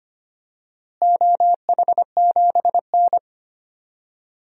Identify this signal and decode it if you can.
Morse code, heard as a keyed tone, about 25 wpm, 700 Hz. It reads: OH7N
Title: DXCC-100 Call Signs - 4 Characters - Repeated 3x - 25wpm